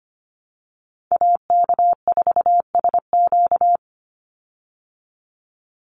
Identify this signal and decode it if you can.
Morse code, heard as a keyed tone, about 25 wpm, 700 Hz. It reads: AK4SQ
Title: Call Signs Repeated 3x - 25wpm